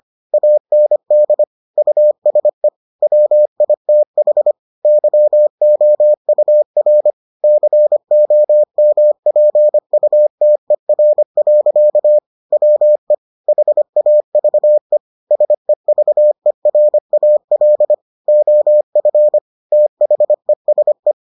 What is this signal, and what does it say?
Morse code, heard as a keyed tone, about 25 wpm, 600 Hz. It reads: AND USE WITH YOUR COMPUTER. WE HAVE SEVERAL OF THESE